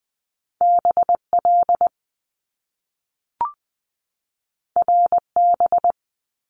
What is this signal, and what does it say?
Morse code, heard as a keyed tone, about 20 wpm, 700 Hz. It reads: BL RB
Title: Top Two Letter Combinations in Words 20wpm